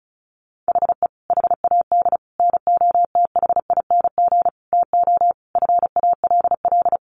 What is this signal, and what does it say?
Morse code, heard as a keyed tone, about 34 wpm, 700 Hz. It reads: HE HAD NOTHING TO FALL
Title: Sentences from Top 1100 Words 35wpm